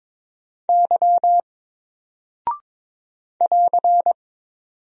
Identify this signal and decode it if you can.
Morse code, heard as a keyed tone, about 22 wpm, 700 Hz. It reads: Y <AR>
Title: Single Character (Letter:Number:Punctuation:Prosign) Alt 22wpm